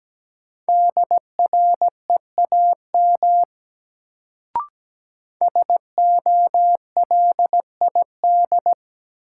Morse code, heard as a keyed tone, about 17 wpm, 700 Hz. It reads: DREAM SOLID
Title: Sentences from Top 1500 Words - New Word Drill Alt - 17wpm